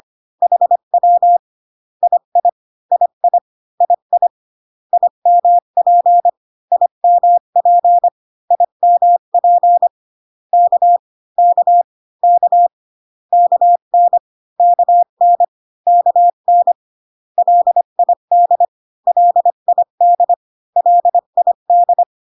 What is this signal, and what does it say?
Morse code, heard as a keyed tone, about 25 wpm, 700 Hz. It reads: HW II II II IMP IMP IMP K K K KN KN KN LID LID LID